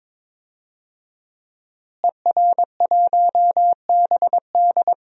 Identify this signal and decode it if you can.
Morse code, heard as a keyed tone, about 22 wpm, 700 Hz. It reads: ER1BD
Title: DXCC-100 Call Signs - 5 Characters - Repeated 3x - 22wpm